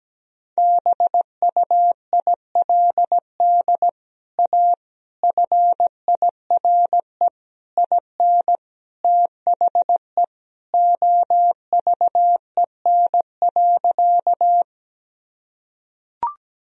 Morse code, heard as a keyed tone, about 17 wpm, 700 Hz. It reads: BUILD A FIRE IN THE OVEN.
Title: Sentences from Top 1000 Words - Review Part 4 - 17wpm